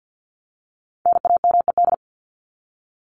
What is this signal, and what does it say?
Morse code, heard as a keyed tone, about 50 wpm, 700 Hz. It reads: NAMED
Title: ED Suffix - Top 100 Words with Base Words 50wpm